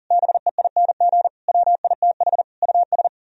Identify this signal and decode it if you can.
Morse code, heard as a keyed tone, about 38 wpm, 700 Hz. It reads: BEING WITH US
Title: Speed-Racing - Sets of 4 Words 40wpm